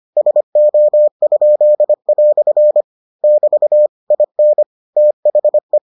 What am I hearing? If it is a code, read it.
Morse code, heard as a keyed tone, about 25 wpm, 600 Hz. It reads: SO?" <BT> IN THE